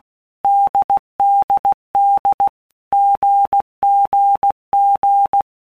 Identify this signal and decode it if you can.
Morse code, heard as a keyed tone, about 16 wpm, 800 Hz. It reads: DDD GGG
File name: All Groups 1-minute Blocks / Groups@15 / AWUNDG